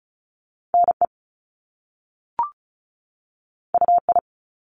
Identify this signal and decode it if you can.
Morse code, heard as a keyed tone, about 36 wpm, 700 Hz. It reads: NE UI
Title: Top Two Letter Combinations in Words 35wpm